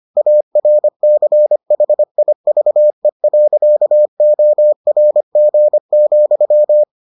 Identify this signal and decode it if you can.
Morse code, heard as a keyed tone, about 25 wpm, 600 Hz. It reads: ARCHIVE.ORG,